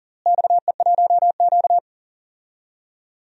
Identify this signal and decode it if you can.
Morse code, heard as a keyed tone, about 40 wpm, 700 Hz. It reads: XE1Q
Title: DXCC-100 Call Signs - 4 Characters - Repeated 2x - 40wpm